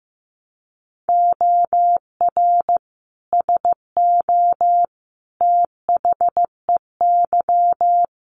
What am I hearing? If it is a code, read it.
Morse code, heard as a keyed tone, about 15 wpm, 700 Hz. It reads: OR SO THEY
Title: Sentences from Top 100 Words 15wpm